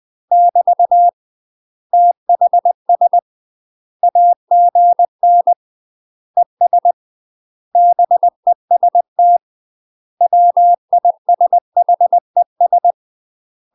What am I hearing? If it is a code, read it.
Morse code, heard as a keyed tone, about 20 wpm, 700 Hz. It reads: <BT> THS AGN ES BEST WISHES
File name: UK MASTER QSOs TRAINING / 20WPM / UK MASTER QSOs 0057